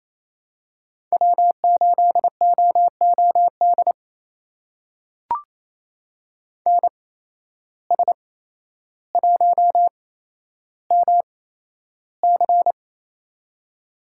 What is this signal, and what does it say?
Morse code, heard as a keyed tone, about 28 wpm, spaced as for 12 wpm, 700 Hz. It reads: W8OOD NS1MC
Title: USA Call Signs - 5 Characters - 8x Character Spacing - 28wpm